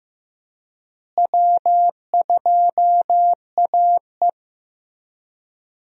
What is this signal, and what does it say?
Morse code, heard as a keyed tone, about 15 wpm, 700 Hz. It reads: W2AE
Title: USA Call Signs - 4 Characters - Repeated 2x - 15wpm